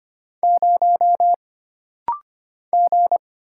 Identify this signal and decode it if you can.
Morse code, heard as a keyed tone, about 25 wpm, 700 Hz. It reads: 0 G
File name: Single Character (Letter:Number:Punctuation:Prosign) - Rapid-Fire - 25wpm